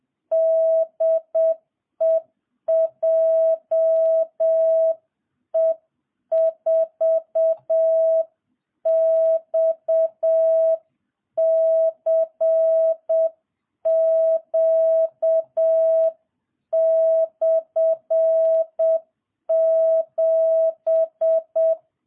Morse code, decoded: DE JE4XCQ/7

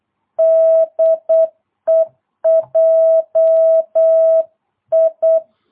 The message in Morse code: DE JI

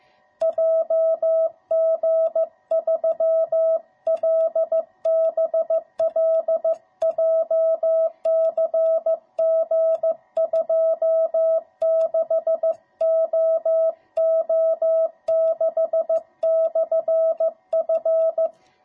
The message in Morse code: JG3LBL JCG26OO6/F

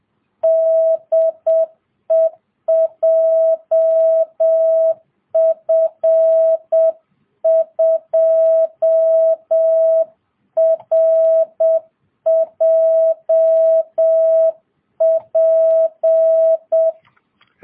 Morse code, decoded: DE JF2RJP

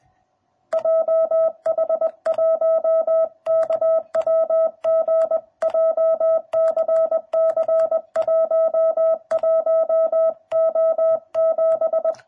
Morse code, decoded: JH1KWG JCC 11O7

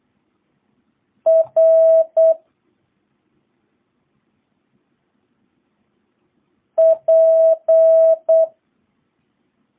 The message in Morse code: R P

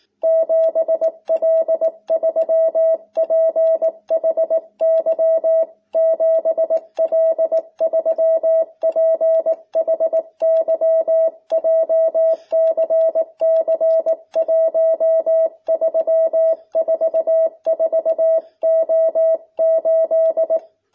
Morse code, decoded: 7L3PHY 7L3PHY JCC1344O8

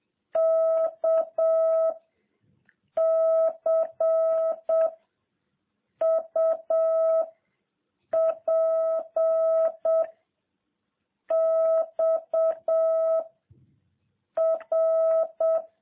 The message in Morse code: K C U P X R

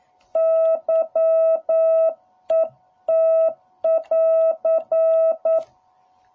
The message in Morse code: YET <AR>

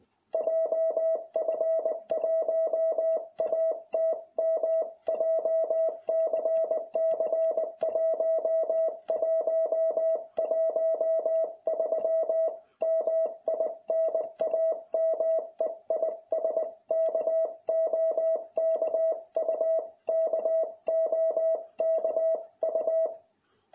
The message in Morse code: JF1ATM JCC 1113 MI NAME IS KOKU KOKU

